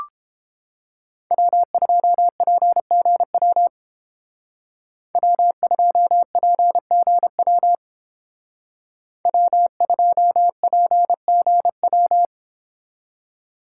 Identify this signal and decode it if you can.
Morse code, heard as a keyed tone, about 30 wpm, 700 Hz. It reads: W2PGW W2PGW W2PGW